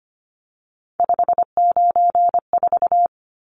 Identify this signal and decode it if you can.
Morse code, heard as a keyed tone, about 25 wpm, 700 Hz. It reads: 594
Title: Numbers - 3-Digit 25wpm